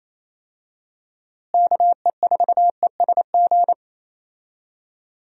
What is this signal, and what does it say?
Morse code, heard as a keyed tone, about 28 wpm, 700 Hz. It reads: KE4ESG
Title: USA Call Signs - 6 Characters - Repeated 3x - 28wpm